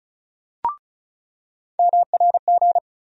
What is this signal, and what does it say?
Morse code, heard as a keyed tone, about 36 wpm, 700 Hz. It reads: MRG
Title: USA Call Sign Postfixes - 3 Letters 35wpm